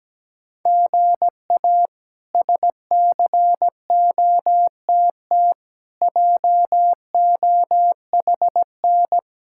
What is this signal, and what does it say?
Morse code, heard as a keyed tone, about 17 wpm, 700 Hz. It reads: GA SCOTT JOHN
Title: K1USN SST QSO Practice - 17wpm